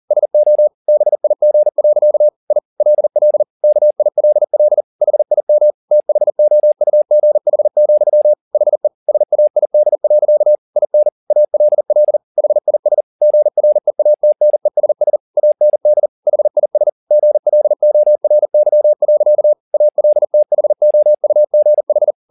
Morse code, heard as a keyed tone, about 40 wpm, 600 Hz. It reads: SO BIG. I LL KILL HIM THOUGH, HE SAID. IN ALL HIS GREATNESS AND HIS GLORY. ALTHOUGH